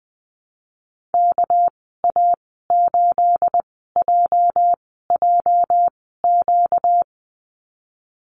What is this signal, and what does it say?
Morse code, heard as a keyed tone, about 20 wpm, spaced as for 15 wpm, 700 Hz. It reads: KA8JJQ
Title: 238 Lesson 38g - Letter q - Call Signs 20wpm